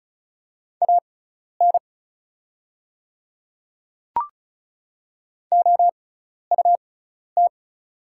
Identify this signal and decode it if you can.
Morse code, heard as a keyed tone, about 36 wpm, spaced as for 12 wpm, 700 Hz. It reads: AN OUT
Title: 6x Character Spacing - Top 100 Words - 35wpm